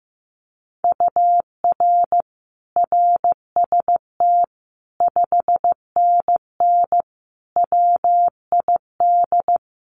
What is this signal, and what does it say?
Morse code, heard as a keyed tone, about 15 wpm, 700 Hz. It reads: UR RST 5NN WID
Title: Common QSO Phrases 15wpm